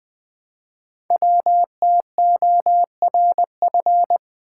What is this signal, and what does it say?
Morse code, heard as a keyed tone, about 20 wpm, 700 Hz. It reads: WTORF
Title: Code Groups - 5 Letter:Numbers - 20wpm